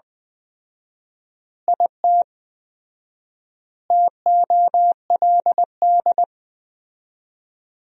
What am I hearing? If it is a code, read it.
Morse code, heard as a keyed tone, about 20 wpm, 700 Hz. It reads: IT TOLD
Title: 4x Word-Spacing - Sentences from Top 400 Words - 20wpm